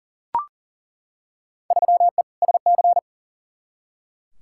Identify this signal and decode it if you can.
Morse code, heard as a keyed tone, about 40 wpm, 700 Hz. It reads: 3E SC
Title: Field Day Exchanges Encore - 40wpm